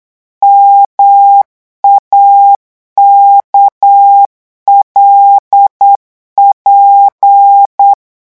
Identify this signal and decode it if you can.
Morse code, heard as a keyed tone, about 8 wpm, 800 Hz. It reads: MAKLP
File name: morse mp3 / revision8-vitesse 8 mots